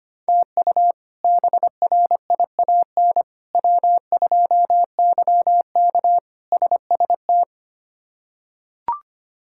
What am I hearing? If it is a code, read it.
Morse code, heard as a keyed tone, about 25 wpm, 700 Hz. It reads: TU BRIAN W2YK SST
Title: K1USN SST QSO Practice - 25wpm